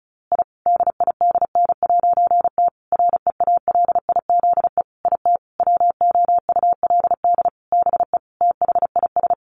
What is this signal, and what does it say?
Morse code, heard as a keyed tone, about 34 wpm, 700 Hz. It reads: I DIDN'T REALIZE IT WOULD BE THIS
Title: A Sound Of Thunder 35wpm